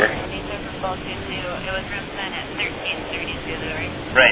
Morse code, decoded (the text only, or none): none